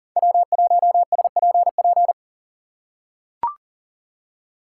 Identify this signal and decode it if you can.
Morse code, heard as a keyed tone, about 40 wpm, 700 Hz. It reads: W1SPP
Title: USA Call Signs - 5 Characters - 4x Character Spacing - 40wpm